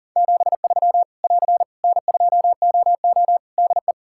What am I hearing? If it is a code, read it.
Morse code, heard as a keyed tone, about 40 wpm, 700 Hz. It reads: 73 <AR> N2OO DE
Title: Practice QSO - Exchange 4 - Ending - 40wpm